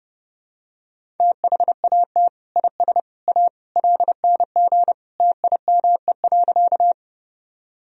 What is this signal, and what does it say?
Morse code, heard as a keyed tone, about 30 wpm, 700 Hz. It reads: THAT IS A LNG TIME.
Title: Sentences from Top 100 Words and Abbreviations 30wpm